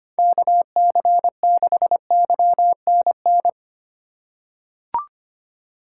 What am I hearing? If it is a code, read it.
Morse code, heard as a keyed tone, about 25 wpm, 700 Hz. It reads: KC6YNN
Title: USA Call Signs - 4x Character Spacing - 25wpm